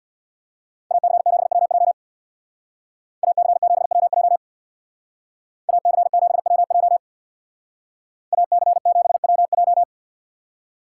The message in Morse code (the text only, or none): AC7WJ AC7WJ AC7WJ AC7WJ